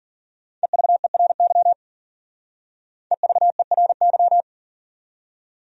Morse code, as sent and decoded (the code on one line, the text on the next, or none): . ...- . .-. -.-- / . ...- . .-. -.--
EVERY EVERY